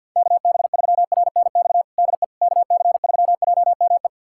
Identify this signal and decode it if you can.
Morse code, heard as a keyed tone, about 52 wpm, 700 Hz. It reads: KB3RNX DE KC3JME
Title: USA Callsign DE Callsign 50wpm